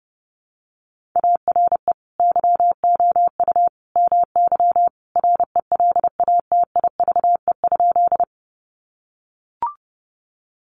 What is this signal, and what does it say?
Morse code, heard as a keyed tone, about 30 wpm, 700 Hz. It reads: ARE YOU MY RELATIVE?
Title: Sentences from Top 1400 Words 30wpm